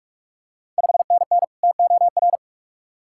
Morse code, heard as a keyed tone, about 46 wpm, 690 Hz. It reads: 5NN TOR
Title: Ontario QSO Party Exchange 45wpm